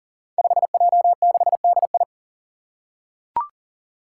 Morse code, heard as a keyed tone, about 39 wpm, 700 Hz. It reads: 5J6DI
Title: Five Challenging Letter-Number Combinations 40wpm